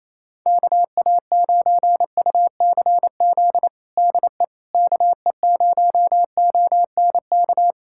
Morse code, heard as a keyed tone, about 28 wpm, 700 Hz. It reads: KA9UCZ DE KE0ONK
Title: USA Callsign DE Callsign - 6 Characters - 28wpm